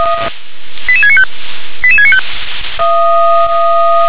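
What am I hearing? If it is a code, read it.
Five-tone selective call (ZVEI), heard as quick runs of five tones, tones of about 70 ms. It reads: 88695 88695
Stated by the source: ZVEI five-tone selective call received on an SDR.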